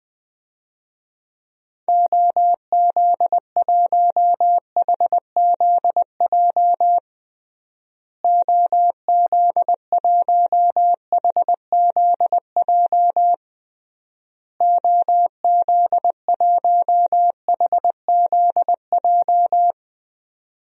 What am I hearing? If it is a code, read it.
Morse code, heard as a keyed tone, about 21 wpm, 700 Hz. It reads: OZ1HZJ OZ1HZJ OZ1HZJ
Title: DXCC-100 Call Signs - 6 Characters - Repeated 3x - 20wpm